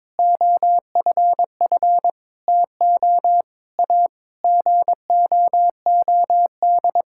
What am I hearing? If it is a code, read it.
Morse code, heard as a keyed tone, about 22 wpm, 700 Hz. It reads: OFF TO A GOOD
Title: Sentences from Top 1600 Words 22wpm